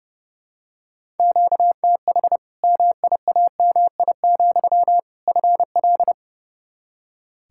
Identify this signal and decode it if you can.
Morse code, heard as a keyed tone, about 30 wpm, 700 Hz. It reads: QTH MIAMI, FL
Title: Practice QSO - Exchange 1 - Beginning - 30wpm